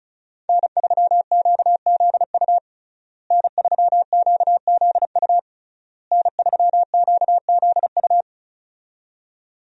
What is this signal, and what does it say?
Morse code, heard as a keyed tone, about 36 wpm, 700 Hz. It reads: N3QZU N3QZU N3QZU
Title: USA Call Signs - 5 Characters - Repeated 3x - 35wpm